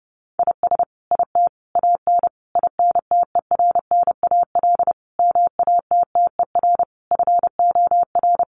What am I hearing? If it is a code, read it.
Morse code, heard as a keyed tone, about 30 wpm, 700 Hz. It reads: IS IT AN INTERNAL MATTER FOR
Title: Sentences from Top 1400 Words 30wpm